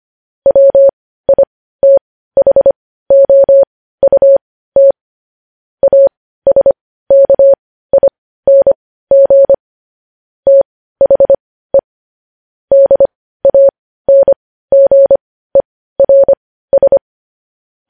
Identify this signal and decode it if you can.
Morse code, heard as a keyed tone, about 25 wpm, spaced as for 15 wpm, 550 Hz. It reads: WITHOUT ASKING THE DANGERS
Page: news headlines in morse code at 15 wpm